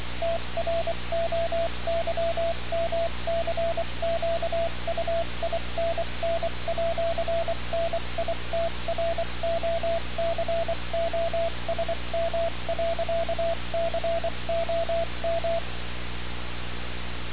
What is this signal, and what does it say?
Morse code, heard as a keyed tone, about 23 wpm, 670 Hz, with radio noise. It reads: TROYMCQUINN@NITROCOSM.COM
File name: radiocap / REC z